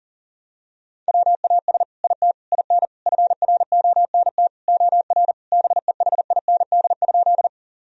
Morse code, heard as a keyed tone, about 40 wpm, 700 Hz. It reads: WAS IT IN FRONT OR BEHIND?
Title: Sentences from Top 1000 Words - Review Part 2 - 40wpm